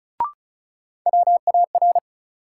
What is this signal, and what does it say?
Morse code, heard as a keyed tone, about 34 wpm, 700 Hz. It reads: WAR
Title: Sets of 5 Words - New Word Drill - 35wpm